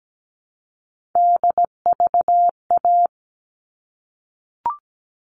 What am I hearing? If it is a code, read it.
Morse code, heard as a keyed tone, about 17 wpm, 700 Hz. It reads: DVA
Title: All Cut Numbers - 3-Digit 17wpm